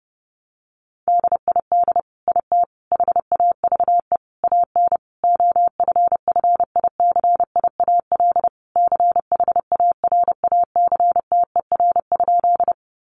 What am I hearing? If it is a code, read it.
Morse code, heard as a keyed tone, about 30 wpm, 700 Hz. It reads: DID IT HAVE AN OFFICIAL CHARACTER?
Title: Sentences from Top 1200 Words 30wpm